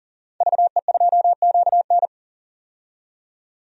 Morse code, heard as a keyed tone, about 37 wpm, 700 Hz. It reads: VE2QN
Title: DXCC-100 Call Signs - 5 Characters - 40wpm